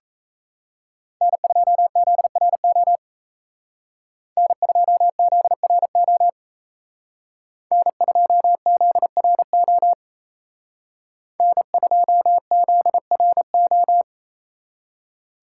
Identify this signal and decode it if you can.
Morse code, heard as a keyed tone, about 35 wpm, 700 Hz. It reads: N2ZRO N2ZRO N2ZRO N2ZRO